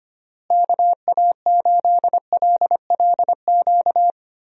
Morse code, heard as a keyed tone, about 25 wpm, 700 Hz. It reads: KA8LLQ
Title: USA Call Signs - 6 Characters - Repeated 3x - 25wpm